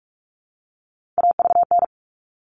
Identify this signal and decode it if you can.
Morse code, heard as a keyed tone, about 44 wpm, 700 Hz. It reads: AVN
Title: All Cut Numbers - 3-Digit 45wpm